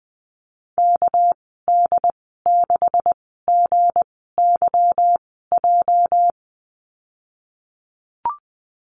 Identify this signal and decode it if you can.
Morse code, heard as a keyed tone, about 20 wpm, spaced as for 14 wpm, 700 Hz. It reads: KD6GYJ
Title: 219 Lesson 35g - Number 0 - Call Signs 20wpm